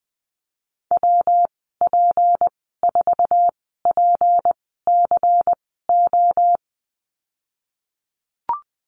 Morse code, heard as a keyed tone, about 19 wpm, spaced as for 14 wpm, 700 Hz. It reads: WP4PCO
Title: 238 Lesson 38g - Letter q - Call Signs 20wpm